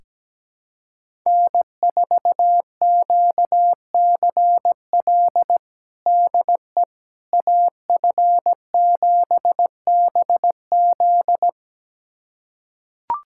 Morse code, heard as a keyed tone, about 17 wpm, 700 Hz. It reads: N4QCL DE AF7BZ
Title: USA Callsign DE Callsign - 5 Characters - 17wpm